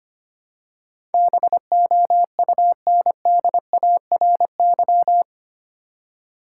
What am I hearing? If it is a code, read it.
Morse code, heard as a keyed tone, about 25 wpm, 700 Hz. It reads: BOUNDARY
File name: 8 Letter Words - First 3 Letters Given - 25wpm